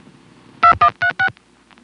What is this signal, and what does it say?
DTMF, heard as telephone touch tones, tones of about 90 ms. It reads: 2136